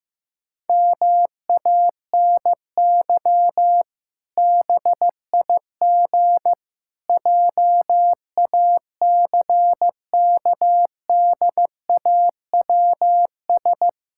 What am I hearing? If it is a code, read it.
Morse code, heard as a keyed tone, about 15 wpm, 700 Hz. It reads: MANY BIG JACKDAWS
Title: Pangram Challenge v2 - 15wpm